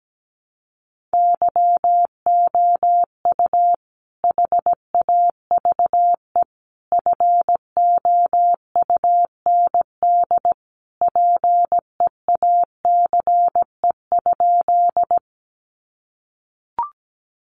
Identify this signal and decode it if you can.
Morse code, heard as a keyed tone, about 17 wpm, 700 Hz. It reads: YOU HAVE FOUND PEACE?
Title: Sentences from Top 1500 Words 17wpm